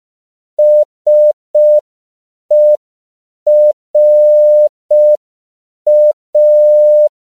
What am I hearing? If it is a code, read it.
Morse code, heard as a keyed tone, about 5 wpm, 600 Hz. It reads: SERA
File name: CW / 5PPM